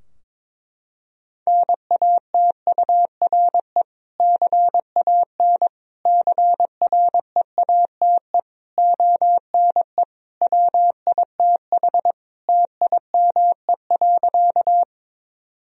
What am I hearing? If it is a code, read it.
Morse code, heard as a keyed tone, about 21 wpm, 700 Hz. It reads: NATURE CAN CREATE ONE WITH TIME.
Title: Sentences from Top 1000 Words - Review Part 3 - 22wpm